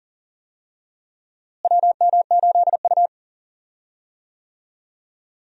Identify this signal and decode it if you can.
Morse code, heard as a keyed tone, about 40 wpm, 700 Hz. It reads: WM8U
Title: USA Call Signs - 4 Characters - Repeated 3x - 40wpm